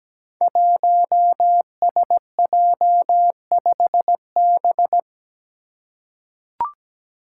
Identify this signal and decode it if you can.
Morse code, heard as a keyed tone, about 17 wpm, 700 Hz. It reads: 1SJ5B
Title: Five Challenging Letter-Number Combinations 17wpm